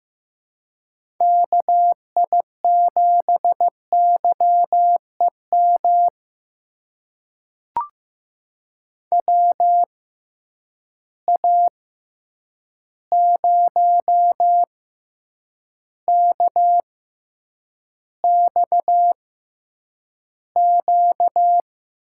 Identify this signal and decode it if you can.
Morse code, heard as a keyed tone, about 16 wpm, spaced as for 8 wpm, 700 Hz. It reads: KI7YEM WA0KXQ
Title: USA Call Signs - 6 Characters - 6x Character Spacing - 15wpm